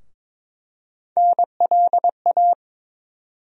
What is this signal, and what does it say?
Morse code, heard as a keyed tone, about 22 wpm, 700 Hz. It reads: NLA
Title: USA Call Sign Postfixes - 3 Letters 22wpm